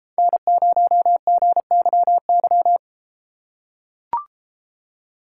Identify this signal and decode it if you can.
Morse code, heard as a keyed tone, about 31 wpm, 700 Hz. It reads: N0GYY